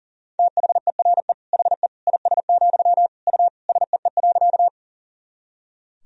Morse code, heard as a keyed tone, about 40 wpm, 700 Hz. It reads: THERE HE IS, U SEE.
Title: Sentences from Top 100 Words and Abbreviations 40wpm